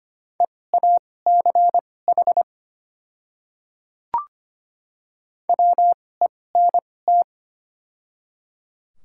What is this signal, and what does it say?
Morse code, heard as a keyed tone, about 25 wpm, spaced as for 18 wpm, 700 Hz. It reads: EACH WENT
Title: 2x Character Spacing - Top 200 Words - 25wpm